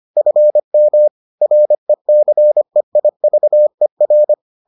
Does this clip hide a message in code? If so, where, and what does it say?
Yes: Morse code from 0.2 to 4.3 s, heard as a keyed tone, about 25 wpm, 600 Hz. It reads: FM RECEIVER